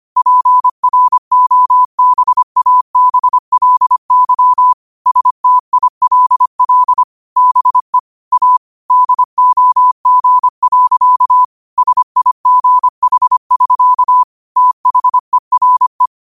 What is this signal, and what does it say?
Morse code, heard as a keyed tone, about 25 wpm, 1000 Hz. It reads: PROBABLY STILL BE A DOG. SIGH<SK> THERE
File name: 25wpm